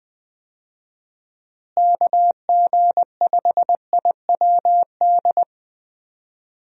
Morse code, heard as a keyed tone, about 20 wpm, 700 Hz. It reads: KG5IWD
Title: USA Call Signs - 6 Characters - Repeated 2x - 20wpm